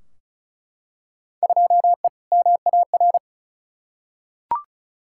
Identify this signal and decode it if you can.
Morse code, heard as a keyed tone, about 36 wpm, 690 Hz. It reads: 2E MAR
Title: Field Day Exchanges Encore - 35wpm